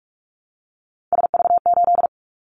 Morse code, heard as a keyed tone, about 46 wpm, 700 Hz. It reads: SV8
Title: Callsign Prefixes for Top 50 DXCC - 45wpm